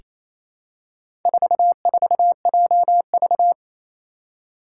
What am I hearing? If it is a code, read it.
Morse code, heard as a keyed tone, about 29 wpm, 700 Hz. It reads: 44JV
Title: Four Challenging Letter-Number Combinations 28wpm